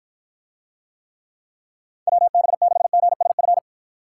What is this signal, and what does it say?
Morse code, heard as a keyed tone, about 54 wpm, 700 Hz. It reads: WB6GSF